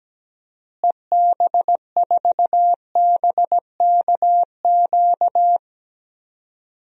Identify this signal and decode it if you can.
Morse code, heard as a keyed tone, about 17 wpm, 700 Hz. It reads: EB4BKQ
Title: DXCC-100 Call Signs - 6 Characters - Repeated 2x - 17wpm